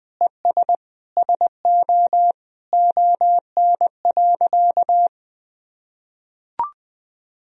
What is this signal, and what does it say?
Morse code, heard as a keyed tone, about 20 wpm, 700 Hz. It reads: ES SO ON.
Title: Sentences from Top 100 Words and Abbreviations 20wpm